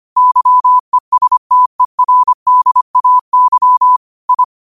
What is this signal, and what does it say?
Morse code, heard as a keyed tone, about 25 wpm, 1000 Hz. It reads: YESTERDAY I
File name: qotdfiles / 25wpm